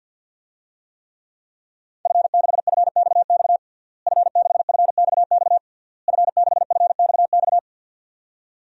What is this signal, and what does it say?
Morse code, heard as a keyed tone, about 53 wpm, 700 Hz. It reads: F6FXX F6FXX F6FXX
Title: DXCC-100 Call Signs - 5 Characters - Repeated 3x - 50wpm